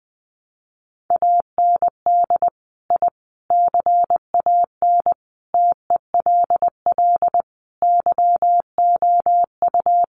Morse code, heard as a keyed tone, about 20 wpm, 700 Hz. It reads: AND I CAN TELL YOU
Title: Sets of 5 Words Encore2 20wpm